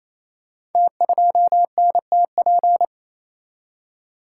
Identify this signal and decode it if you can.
Morse code, heard as a keyed tone, about 29 wpm, 700 Hz. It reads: T2NTP
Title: Code Groups - 5 Letter:Numbers - 28wpm